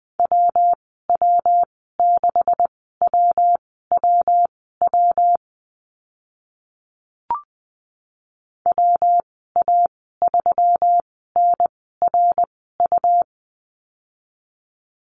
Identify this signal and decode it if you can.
Morse code, heard as a keyed tone, about 20 wpm, spaced as for 14 wpm, 700 Hz. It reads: WW6WWW WA3NRU
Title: 140 Lesson 23g - Letter f - Call Signs 20wpm